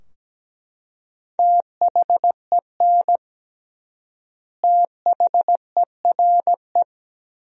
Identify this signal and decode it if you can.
Morse code, heard as a keyed tone, about 17 wpm, 700 Hz. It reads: THEN THERE
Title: 3x Word-Spacing - Sentences from Top 100 Words - 17wpm